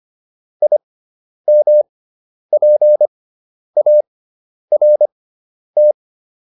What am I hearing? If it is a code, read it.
Morse code, heard as a keyed tone, about 25 wpm, spaced as for 10 wpm, 600 Hz. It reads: IMPART